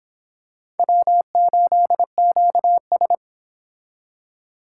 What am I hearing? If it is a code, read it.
Morse code, heard as a keyed tone, about 26 wpm, 700 Hz. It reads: W8QS